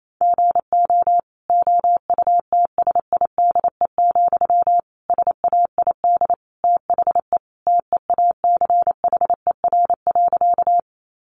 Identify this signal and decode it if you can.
Morse code, heard as a keyed tone, about 28 wpm, 700 Hz. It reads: GO OUTSIDE, SAID THE TEACHER.
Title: Sentences from Top 900 Words 28wpm